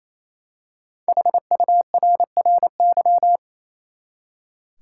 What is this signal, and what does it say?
Morse code, heard as a keyed tone, about 28 wpm, 700 Hz. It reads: HURRY
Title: Five Letter Words - 28wpm